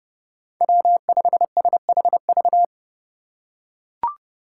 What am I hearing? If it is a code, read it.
Morse code, heard as a keyed tone, about 30 wpm, 700 Hz. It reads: W5SHV
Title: USA Call Signs - 5 Characters - 2x Character Spacing - 30wpm